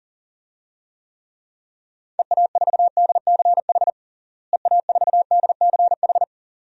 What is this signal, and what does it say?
Morse code, heard as a keyed tone, about 40 wpm, 700 Hz. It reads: EA4DCH EA4DCH
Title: DXCC-100 Call Signs - 6 Characters - Repeated 2x - 40wpm